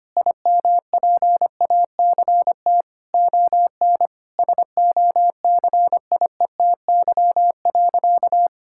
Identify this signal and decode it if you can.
Morse code, heard as a keyed tone, about 25 wpm, 700 Hz. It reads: IMPACT ON SOCIETY.
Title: Sentences from Top 1000 Words - Review Part 6 - 25wpm